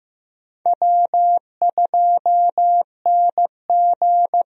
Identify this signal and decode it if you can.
Morse code, heard as a keyed tone, about 15 wpm, 700 Hz. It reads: W2NG
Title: USA Call Signs - 4 Characters - Repeated 3x - 15wpm